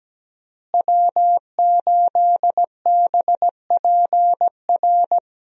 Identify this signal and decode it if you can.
Morse code, heard as a keyed tone, about 17 wpm, 700 Hz. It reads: W8BPR
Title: USA Call Signs - 5 Characters - Repeated 3x - 17wpm